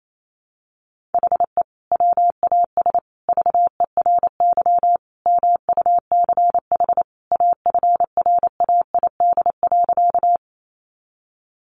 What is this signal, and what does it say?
Morse code, heard as a keyed tone, about 28 wpm, 700 Hz. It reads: HE WAS VERY MUCH AFRAID.
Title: Sentences from Top 1500 Words 28wpm